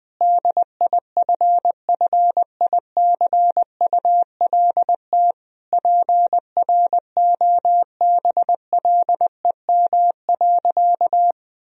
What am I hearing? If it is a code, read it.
Morse code, heard as a keyed tone, about 20 wpm, 700 Hz. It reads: DIFFICULT PROBLEM.
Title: Sentences from Top 1000 Words - Review Part 2 - 20wpm